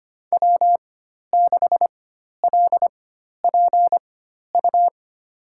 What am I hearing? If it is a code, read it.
Morse code, heard as a keyed tone, about 25 wpm, spaced as for 12 wpm, 700 Hz. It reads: W6LPU
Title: USA Call Signs - 5 Characters - 4x Character Spacing - 25wpm